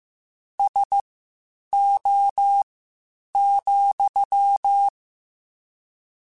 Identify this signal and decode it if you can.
Morse code, heard as a keyed tone, about 15 wpm, spaced as for 8 wpm, 800 Hz. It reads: SO,
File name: QSOs / QSOs-mp3s / WD0BGP 1@10